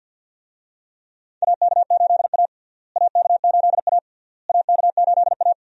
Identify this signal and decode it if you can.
Morse code, heard as a keyed tone, about 47 wpm, 700 Hz. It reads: AK8A AK8A AK8A
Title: USA Call Signs - 4 Characters - Repeated 3x - 50wpm